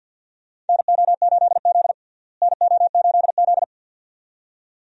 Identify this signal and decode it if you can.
Morse code, heard as a keyed tone, about 52 wpm, 700 Hz. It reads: NO8Z NO8Z
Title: USA Call Signs - 4 Characters - Repeated 2x - 50wpm